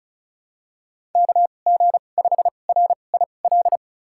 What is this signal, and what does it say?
Morse code, heard as a keyed tone, about 34 wpm, spaced as for 25 wpm, 700 Hz. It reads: KG5RIL
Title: USA Call Signs - 2x Character Spacing - 35wpm